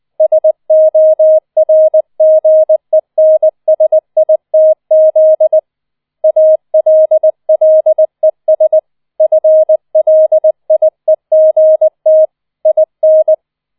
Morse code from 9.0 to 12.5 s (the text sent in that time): FLIEGT